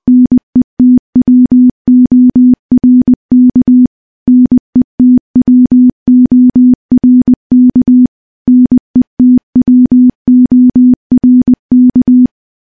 Morse code, heard as a keyed tone, about 20 wpm, 250 Hz. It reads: NETWORK NETWORK NETWORK
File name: morse-training-20WPM